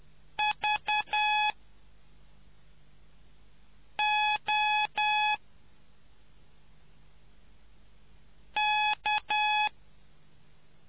Morse code, heard as a keyed tone, about 10 wpm, spaced as for 3 wpm, 850 Hz. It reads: VOK